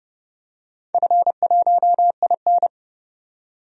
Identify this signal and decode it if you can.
Morse code, heard as a keyed tone, about 30 wpm, 700 Hz. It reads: F1IN
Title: DXCC-100 Call Signs - 4 Characters - 30wpm